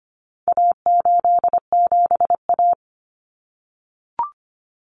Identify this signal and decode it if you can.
Morse code, heard as a keyed tone, about 25 wpm, 700 Hz. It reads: A87A
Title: Common Cut Numbers - 4-Digit 25wpm